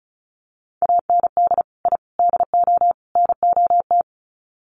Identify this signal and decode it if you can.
Morse code, heard as a keyed tone, about 34 wpm, 700 Hz. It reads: AND I DO NOT
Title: Sets of 4 Words 35wpm